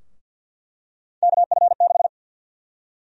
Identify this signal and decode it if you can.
Morse code, heard as a keyed tone, about 49 wpm, 700 Hz. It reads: KF6
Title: USA Call Sign Prefixes - 50wpm